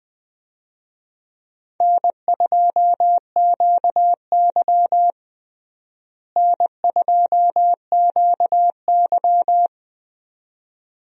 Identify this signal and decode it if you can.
Morse code, heard as a keyed tone, about 20 wpm, 700 Hz. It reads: N2QY N2QY
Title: USA Call Signs - 4 Characters - Repeated 2x - 20wpm